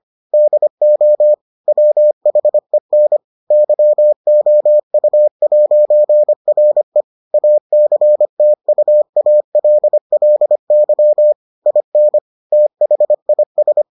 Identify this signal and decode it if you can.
Morse code, heard as a keyed tone, about 25 wpm, 600 Hz. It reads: DO WHEN YOU'RE ACTUALLY IN THIS